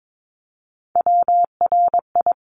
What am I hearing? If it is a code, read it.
Morse code, heard as a keyed tone, about 22 wpm, 700 Hz. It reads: WRI
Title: ING Suffix - Top 100 Words with Base Words 22wpm